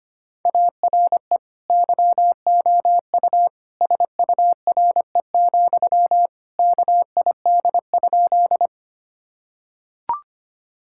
Morse code, heard as a keyed tone, about 25 wpm, 700 Hz. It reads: ARE YOU SURE, KID?
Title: Sentences from Top 1600 Words 25wpm